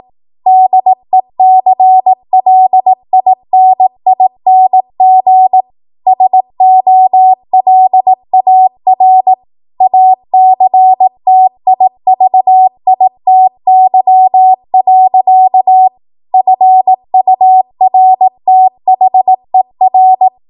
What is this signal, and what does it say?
Morse code, heard as a keyed tone, about 18 wpm, 750 Hz. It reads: DECLINING SOLAR ACTIVITY. FURTHER
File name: Morse / 18WPM